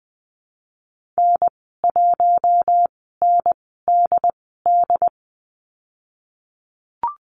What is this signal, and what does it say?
Morse code, heard as a keyed tone, about 20 wpm, spaced as for 14 wpm, 700 Hz. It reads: N1NDD
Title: 069 Lesson 12g - Letter d - Call Signs 20wpm